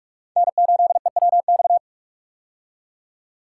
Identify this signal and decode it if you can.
Morse code, heard as a keyed tone, about 44 wpm, 700 Hz. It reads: N8EWX
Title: K1USN SST QSO Practice - 45wpm